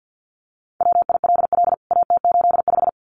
Morse code, heard as a keyed tone, about 50 wpm, 700 Hz. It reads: WILL AT85
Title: CWT Exchange 50wpm